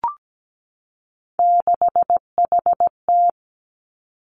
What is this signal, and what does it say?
Morse code, heard as a keyed tone, about 17 wpm, 700 Hz. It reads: 6HT
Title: Three Challenging Letter-Number Combinations 17wpm